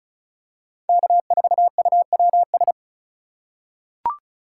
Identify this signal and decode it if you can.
Morse code, heard as a keyed tone, about 36 wpm, 700 Hz. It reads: K4UWS
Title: USA Call Signs - 5 Characters - 4x Character Spacing - 35wpm